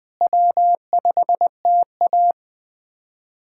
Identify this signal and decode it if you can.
Morse code, heard as a keyed tone, about 20 wpm, 700 Hz. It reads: W5TA
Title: USA Callsign DE Callsign - 4 Characters - 20wpm